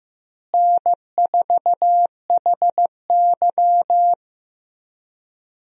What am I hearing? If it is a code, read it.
Morse code, heard as a keyed tone, about 16 wpm, 700 Hz. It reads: N4HY
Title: USA Call Signs - 4 Characters - Repeated 2x - 15wpm